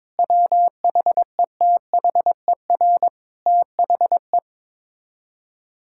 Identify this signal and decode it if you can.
Morse code, heard as a keyed tone, about 22 wpm, 700 Hz. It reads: WHETHER THE
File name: practice / 22wpm / Sets of 7 Words - 22wpm